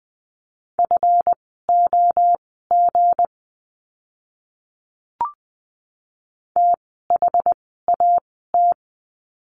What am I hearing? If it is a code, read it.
Morse code, heard as a keyed tone, about 20 wpm, spaced as for 14 wpm, 700 Hz. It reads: FOG THAT
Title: 160 Lesson 26f - Letter g - Words 20wpm